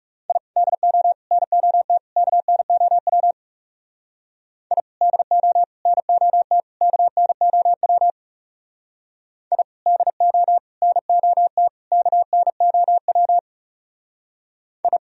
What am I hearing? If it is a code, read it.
Morse code, heard as a keyed tone, about 35 wpm, 690 Hz. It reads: I DO NOT KNOW I DO NOT KNOW I DO NOT KNOW I
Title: Speed-Racing - Sets of 4 Words 30wpm